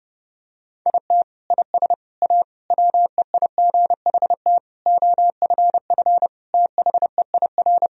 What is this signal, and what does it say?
Morse code, heard as a keyed tone, about 30 wpm, 700 Hz. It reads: IT IS A WEIGHT OFF THEIR
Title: Sentences from Top 1000 Words - Review Part 3 - 30wpm